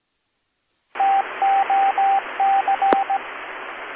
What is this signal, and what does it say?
Morse code, heard as a keyed tone, about 18 wpm, 780 Hz, with radio noise. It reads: TOC